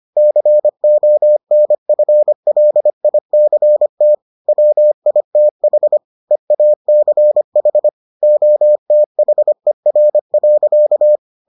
Morse code, heard as a keyed tone, about 25 wpm, 600 Hz. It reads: CONFLICT WITH EACH OTHER.